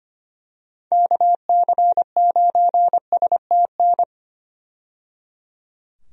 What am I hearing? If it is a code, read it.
Morse code, heard as a keyed tone, about 25 wpm, 700 Hz. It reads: KC9STN
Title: USA Call Signs - 6 Characters - 25wpm